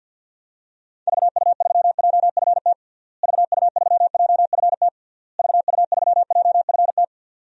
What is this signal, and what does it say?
Morse code, heard as a keyed tone, about 52 wpm, 700 Hz. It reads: VU3JFT VU3JFT VU3JFT
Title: DXCC-100 Call Signs - 6 Characters - Repeated 3x - 50wpm